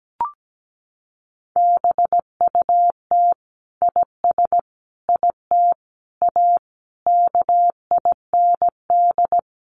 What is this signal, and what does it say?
Morse code, heard as a keyed tone, about 17 wpm, 700 Hz. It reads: BUT IS IT A KIND
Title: Sentences from Top 1200 Words 17wpm